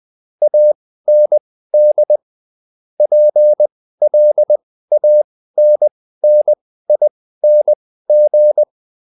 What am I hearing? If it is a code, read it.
Morse code, heard as a keyed tone, about 20 wpm, spaced as for 14 wpm, 600 Hz. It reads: AND PLANNING